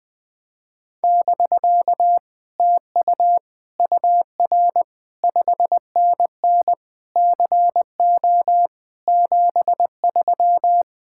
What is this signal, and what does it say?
Morse code, heard as a keyed tone, about 20 wpm, 700 Hz. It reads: <BK> TU UR 5NN CO 73
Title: Speed-Racing Common QSO Phrases - 17wpm